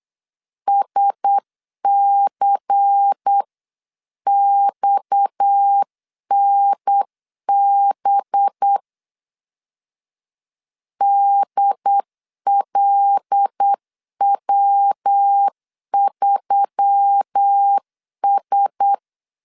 Morse code, heard as a keyed tone, about 8 wpm, 800 Hz. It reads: SCXNB DLW3S
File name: CW / DET / 1 6 1500C S C 8wpm 5219